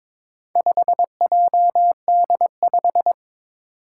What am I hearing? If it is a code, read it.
Morse code, heard as a keyed tone, about 22 wpm, 700 Hz. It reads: 5JD5
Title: Four Challenging Letter-Number Combinations 22wpm